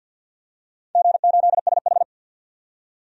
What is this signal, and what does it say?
Morse code, heard as a keyed tone, about 53 wpm, 700 Hz. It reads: G8SH
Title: DXCC-100 Call Signs - 4 Characters - 50wpm